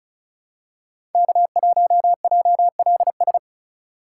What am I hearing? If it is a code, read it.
Morse code, heard as a keyed tone, about 36 wpm, 700 Hz. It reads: K1JLS
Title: USA Call Signs - 5 Characters - 4x Character Spacing - 35wpm